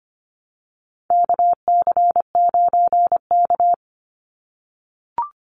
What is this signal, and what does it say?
Morse code, heard as a keyed tone, about 25 wpm, 700 Hz. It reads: KC9K
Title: Call Signs - 25wpm